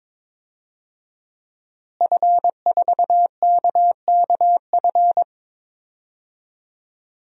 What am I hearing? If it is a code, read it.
Morse code, heard as a keyed tone, about 21 wpm, 700 Hz. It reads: F4KKF
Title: DXCC-100 Call Signs - 5 Characters - Repeated 2x - 22wpm